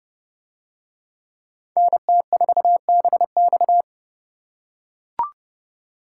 Morse code, heard as a keyed tone, about 28 wpm, 700 Hz. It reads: NT4BX